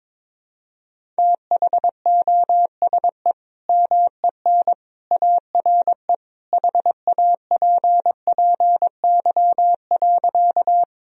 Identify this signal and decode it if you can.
Morse code, heard as a keyed tone, about 21 wpm, 700 Hz. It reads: THOSE MEN ARE HAPPY.
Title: Sentences from Top 1000 Words - Review Part 1 - 22wpm